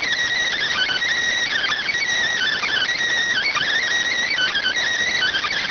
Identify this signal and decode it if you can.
Emergency Alert System (SAME) tones (header bursts), heard as harsh buzzing bursts. It is heard from 0.5 to 4.8 s.